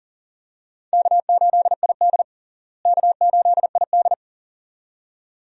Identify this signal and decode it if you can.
Morse code, heard as a keyed tone, about 40 wpm, 700 Hz. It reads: K8ID K8ID
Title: USA Call Signs - 4 Characters - Repeated 2x - 40wpm